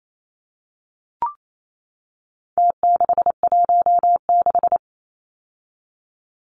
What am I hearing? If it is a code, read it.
Morse code, heard as a keyed tone, about 28 wpm, 700 Hz. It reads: T616
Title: Numbers - Contest Style - 4-Digit 28wpm